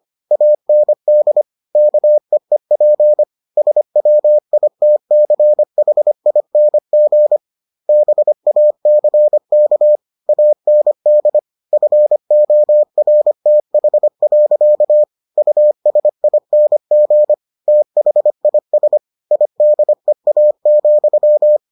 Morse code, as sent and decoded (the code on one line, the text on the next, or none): .- -. -.. / -.- . .--. / ... .-- .. - -.-. .... .. -. --. / -... .- -.-. -.- / .- -. -.. / ..-. --- .-. - .... .-.-.- / ..- ... .. -. --. / - .... .. ... / .. -.. .- --..--
AND KEEP SWITCHING BACK AND FORTH. USING THIS IDEA,